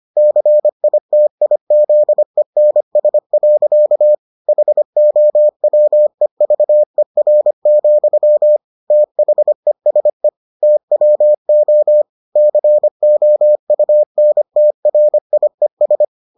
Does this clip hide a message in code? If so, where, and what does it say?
Yes: Morse code from 0.2 to 16.0 s, heard as a keyed tone, about 25 wpm, 600 Hz. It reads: CITIZENS. HOWEVER, THESE TWO COUNTRIES